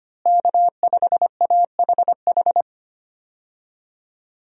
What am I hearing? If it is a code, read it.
Morse code, heard as a keyed tone, about 25 wpm, 700 Hz. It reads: K5AHH
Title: USA Call Signs - 5 Characters - Repeated 3x - 25wpm